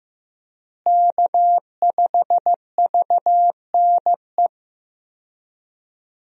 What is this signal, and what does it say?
Morse code, heard as a keyed tone, about 16 wpm, 700 Hz. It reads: K5VNE
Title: USA Call Signs - 5 Characters - Repeated 2x - 15wpm